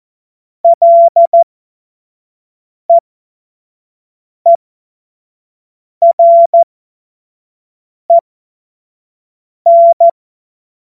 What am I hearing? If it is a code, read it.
Morse code, heard as a keyed tone, about 14 wpm, spaced as for 5 wpm, 680 Hz. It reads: LEEREN
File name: Morsekurs / Witze / Witz 04. Mai 2021, 25 F70